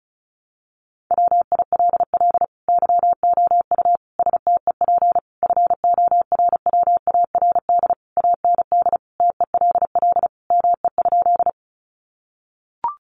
Morse code, heard as a keyed tone, about 34 wpm, 700 Hz. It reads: WILL YOU STEP FORWARD AND TELL ME?
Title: Sentences from Top 1300 Words 35wpm